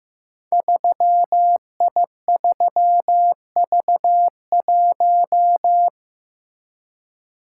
Morse code, heard as a keyed tone, about 15 wpm, 700 Hz. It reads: 3I3V1
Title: Code Groups - 5 Letter:Numbers - 15wpm